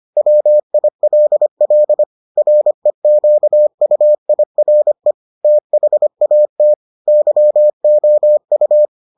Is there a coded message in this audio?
Yes: Morse code from 0.2 to 8.9 s, heard as a keyed tone, about 25 wpm, 600 Hz. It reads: WILL REQUIRE THAT YOU